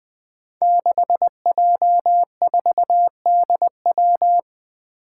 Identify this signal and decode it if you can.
Morse code, heard as a keyed tone, about 20 wpm, 700 Hz. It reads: 6J4DW
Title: Five Challenging Letter-Number Combinations 20wpm